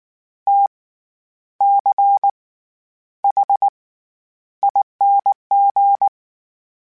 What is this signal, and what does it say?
Morse code, heard as a keyed tone, about 19 wpm, spaced as for 10 wpm, 800 Hz. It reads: TCHING
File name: Special INGs / Special INGs-mp3s / SlowToFaster ing @18